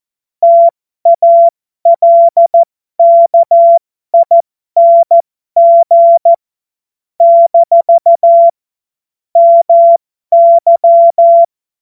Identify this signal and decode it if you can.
Morse code, heard as a keyed tone, about 14 wpm, spaced as for 12 wpm, 680 Hz. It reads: TALKING - MY
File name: Morsekurs / Witze